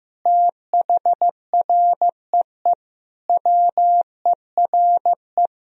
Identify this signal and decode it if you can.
Morse code, heard as a keyed tone, about 15 wpm, 700 Hz. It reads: THREE WERE
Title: Sentences from Top 1000 Words - Review Part 1 - 15wpm